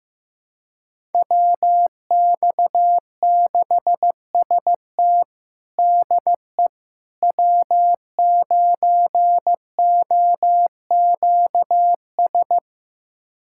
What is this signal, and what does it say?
Morse code, heard as a keyed tone, about 15 wpm, 700 Hz. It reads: WX6ST DE W9OQS
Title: USA Callsign DE Callsign - 5 Characters - 15wpm